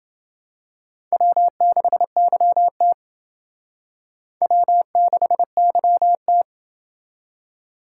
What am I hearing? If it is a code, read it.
Morse code, heard as a keyed tone, about 28 wpm, 700 Hz. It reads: W6YT W6YT